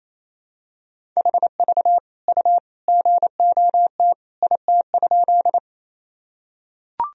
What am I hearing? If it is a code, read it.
Morse code, heard as a keyed tone, about 28 wpm, 700 Hz. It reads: HV U GOT IT?
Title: Sentences from Top 300 Words and Abbreviations 28wpm